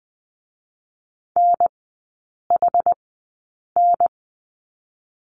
Morse code, heard as a keyed tone, about 20 wpm, spaced as for 8 wpm, 690 Hz. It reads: NHN
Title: 087 Lesson 15e - Number 5 - Three Characters 20wpm